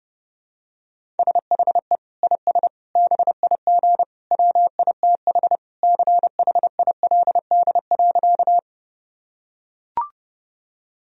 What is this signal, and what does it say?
Morse code, heard as a keyed tone, about 30 wpm, 700 Hz. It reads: SHE IS BIG WITH CHILD.
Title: Sentences from Top 1000 Words - Review Part 1 - 30wpm